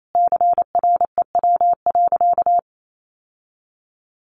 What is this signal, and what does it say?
Morse code, heard as a keyed tone, about 28 wpm, 700 Hz. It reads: CREW.
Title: Sentences from Top 1500 Words 28wpm